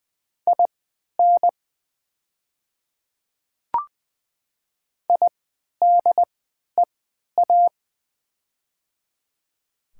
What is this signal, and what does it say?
Morse code, heard as a keyed tone, about 20 wpm, spaced as for 11 wpm, 700 Hz. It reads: IN IDEA
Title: 3x Character Spacing - Top 300 Words - 20wpm